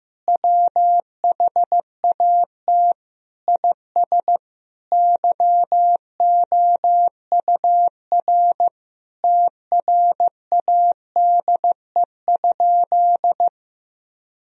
Morse code, heard as a keyed tone, about 15 wpm, 700 Hz. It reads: WHAT IS YOUR TRADE?
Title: Sentences from Top 1000 Words - Review Part 4 - 15wpm